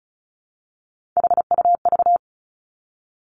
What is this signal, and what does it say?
Morse code, heard as a keyed tone, about 36 wpm, 700 Hz. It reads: HUV
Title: Three Challenging Letter-Number Combinations 35wpm